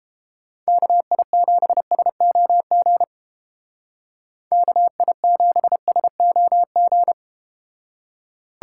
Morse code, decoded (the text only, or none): KI7SOG KI7SOG